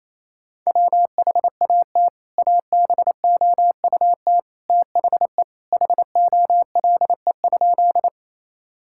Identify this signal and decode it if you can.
Morse code, heard as a keyed tone, about 28 wpm, 700 Hz. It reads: WHAT ABOUT THE HOLE?
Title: Sentences from Top 1600 Words 28wpm